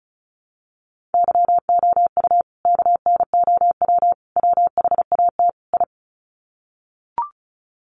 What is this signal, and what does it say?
Morse code, heard as a keyed tone, about 34 wpm, 700 Hz. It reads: YOU KNOW WHAT I
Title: Sets of 4 Words 35wpm